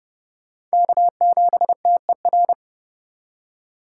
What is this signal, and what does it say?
Morse code, heard as a keyed tone, about 37 wpm, 700 Hz. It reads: K7TER
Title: Speed Racing - USA Call Signs 30wpm